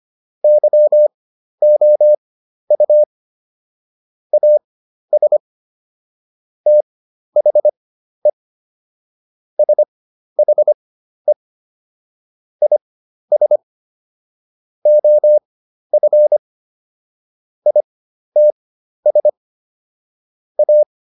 Morse code, heard as a keyed tone, about 25 wpm, spaced as for 12 wpm, 600 Hz. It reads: YOU AS THE SHE IS OF ITS A